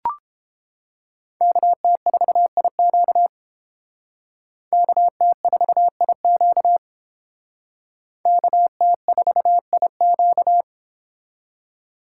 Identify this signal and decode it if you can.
Morse code, heard as a keyed tone, about 29 wpm, 700 Hz. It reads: KT4IQ KT4IQ KT4IQ